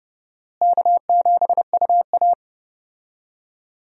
Morse code, heard as a keyed tone, about 31 wpm, 700 Hz. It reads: K7UA
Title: USA Call Signs - 4 Characters - Repeated 2x - 30wpm